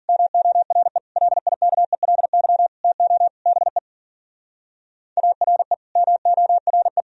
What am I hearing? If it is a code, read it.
Morse code, heard as a keyed tone, about 49 wpm, 700 Hz. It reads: MORE LIKELY TO BE ARE MORE
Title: Speed-Racing - Sets of 5 Words 40wpm